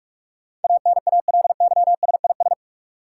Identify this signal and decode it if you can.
Morse code, heard as a keyed tone, about 46 wpm, 700 Hz. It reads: ANALYSIS
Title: Sentences from Top 1400 Words - New Word Drill Alt - 45wpm